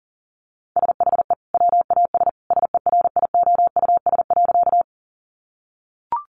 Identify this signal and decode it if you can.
Morse code, heard as a keyed tone, about 40 wpm, 700 Hz. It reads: SHE WAS SERIOUS.
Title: Sentences from Top 1100 Words 40wpm